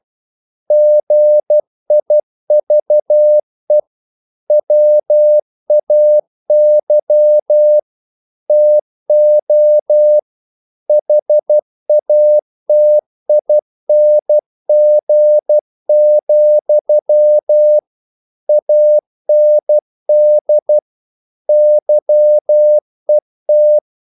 Morse code, heard as a keyed tone, about 12 wpm, 600 Hz. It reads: GIVE WAY TO HATING, AND YET
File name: CW / Practice / If12